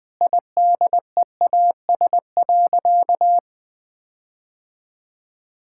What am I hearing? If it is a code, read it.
Morse code, heard as a keyed tone, about 20 wpm, 700 Hz. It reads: IDEAS.
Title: Sentences with 4-Word Phrases - Spoken Beginning - 20wpm